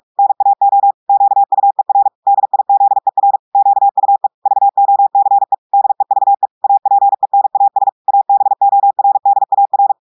Morse code, heard as a keyed tone, about 46 wpm, 800 Hz. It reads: NAO QUER DIZER QUE VOCE DEVE APENAS ABORDAR
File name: audios / portugais / portugais texte en clair 2 vitesse 45 mots minutes